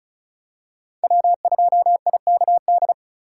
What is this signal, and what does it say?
Morse code, heard as a keyed tone, about 36 wpm, 690 Hz. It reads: W2IKD
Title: USA Call Signs - 5 Characters - 3x Character Spacing - 35wpm